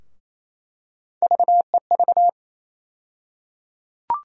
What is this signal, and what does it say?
Morse code, heard as a keyed tone, about 28 wpm, 700 Hz. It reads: VEV
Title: USA Call Sign Postfixes - 3 Letters 28wpm